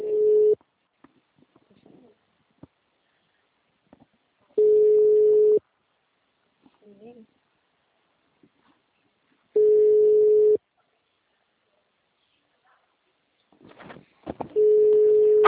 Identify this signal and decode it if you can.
Ringback tone (Europe, 425 Hz), with speech.